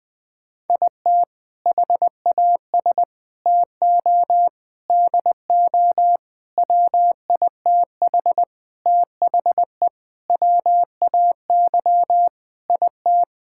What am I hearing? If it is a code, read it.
Morse code, heard as a keyed tone, about 20 wpm, 700 Hz. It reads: IT HAS TO DO WITH THE WAY IT
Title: Sentences with 5-Word Phrases - 20wpm